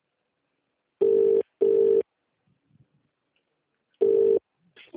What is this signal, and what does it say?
Ringback tone (UK), 400 + 450 Hz.